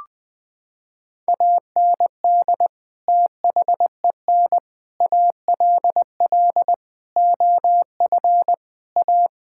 Morse code, heard as a keyed tone, about 21 wpm, 700 Hz. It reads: AND THEN ALL OF A
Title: Sets of 6 Words - 20wpm